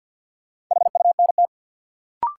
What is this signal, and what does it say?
Morse code, heard as a keyed tone, about 52 wpm, 700 Hz. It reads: HUNT
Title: Sentences from Top 1600 Words - New Word Drill - 50wpm